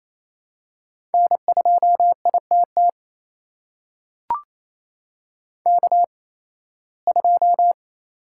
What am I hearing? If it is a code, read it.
Morse code, heard as a keyed tone, about 28 wpm, spaced as for 18 wpm, 700 Hz. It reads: N2ITT K2
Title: USA Call Signs - 8x Character Spacing - 28wpm